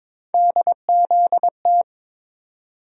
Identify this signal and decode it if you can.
Morse code, heard as a keyed tone, about 22 wpm, 700 Hz. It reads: DZT
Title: USA Call Sign Postfixes - 3 Letters 22wpm